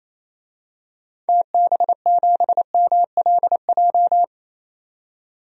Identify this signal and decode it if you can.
Morse code, heard as a keyed tone, about 28 wpm, 700 Hz. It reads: TB7MLJ
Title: DXCC-100 Call Signs - 6 Characters - 28wpm